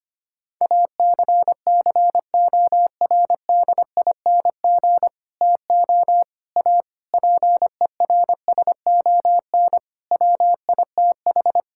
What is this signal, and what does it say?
Morse code, heard as a keyed tone, about 25 wpm, 700 Hz. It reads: ACCORDING TO A PERSON WITH
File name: practice / 25wpm / Sentences with 9-Word Phrases - 25wpm